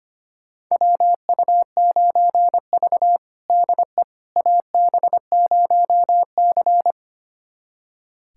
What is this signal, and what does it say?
Morse code, heard as a keyed tone, about 25 wpm, 700 Hz. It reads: WU9V DE AB0C
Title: USA Callsign DE Callsign - 4 Characters - 25wpm